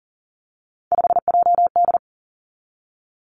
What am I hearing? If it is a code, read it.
Morse code, heard as a keyed tone, about 42 wpm, 700 Hz. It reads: 5JD
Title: Three Challenging Letter-Number Combinations 40wpm